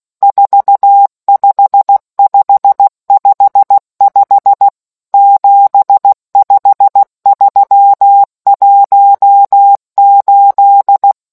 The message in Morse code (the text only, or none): 45555 75318